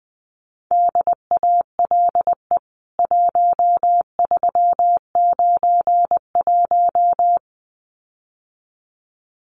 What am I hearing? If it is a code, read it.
Morse code, heard as a keyed tone, about 20 wpm, 700 Hz. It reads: DALE 1391
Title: CWT Exchange 20wpm